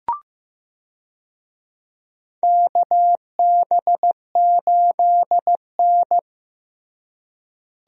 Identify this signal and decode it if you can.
Morse code, heard as a keyed tone, about 15 wpm, 700 Hz. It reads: KB8N
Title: USA Call Signs - 4 Characters - Repeated 3x - 15wpm